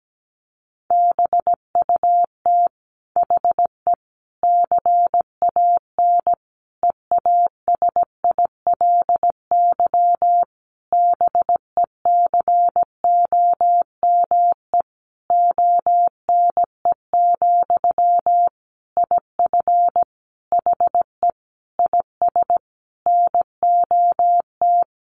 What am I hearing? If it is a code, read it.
Morse code, heard as a keyed tone, about 17 wpm, 700 Hz. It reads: BUT HE CAN EASILY BECOME ONE, IF HE IS NOT
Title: Sentences from Top 900 Words 17wpm